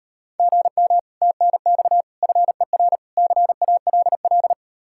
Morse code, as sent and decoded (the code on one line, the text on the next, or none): --. -- / - -. -..- / ..-. .-. / -.-. .- .-.. .-..
GM TNX FER CALL